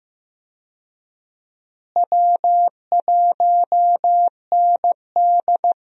Morse code, heard as a keyed tone, about 15 wpm, 700 Hz. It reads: W1ND
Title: USA Call Signs - 4 Characters - Repeated 2x - 15wpm